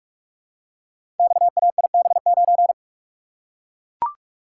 Morse code, heard as a keyed tone, about 44 wpm, 700 Hz. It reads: XAIB9
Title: Code Groups - 5 Letter:Numbers - 45wpm